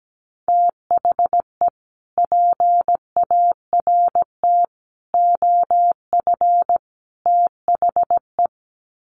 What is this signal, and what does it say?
Morse code, heard as a keyed tone, about 17 wpm, 700 Hz. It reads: THE PART OF THE
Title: Sets of 4 Words 17wpm